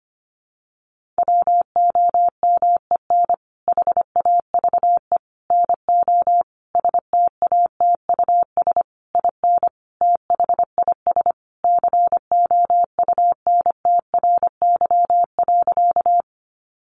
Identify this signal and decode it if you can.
Morse code, heard as a keyed tone, about 25 wpm, 700 Hz. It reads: WOMEN HAVE NO STATUS IN THIS COUNTRY.
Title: Sentences from Top 1400 Words 25wpm